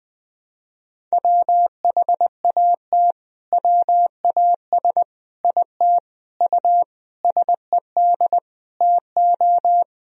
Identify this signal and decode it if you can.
Morse code, heard as a keyed tone, about 20 wpm, 700 Hz. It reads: WHAT WAS IT U SED TO
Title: Sentences from Top 200 Words and Abbreviations 20wpm